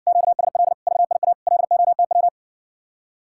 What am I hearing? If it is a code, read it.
Morse code, heard as a keyed tone, about 67 wpm, 700 Hz. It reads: QSL VIA LOTW